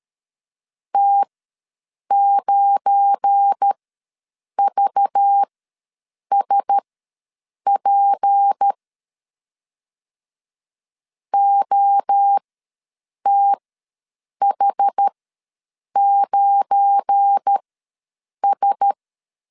Morse code, heard as a keyed tone, about 13 wpm, spaced as for 7 wpm, 800 Hz. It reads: T9VSP OTH9S
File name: LEZIONI / 7 GF9Z / 7 GF9 ws 12wpm 3344